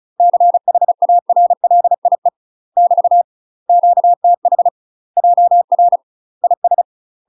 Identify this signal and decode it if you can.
Morse code, heard as a keyed tone, about 35 wpm, 700 Hz. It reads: CHARLIE <BT> QTH JR IS
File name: UK MASTER QSOs TRAINING / 35WPM / UK MASTER QSOs 0036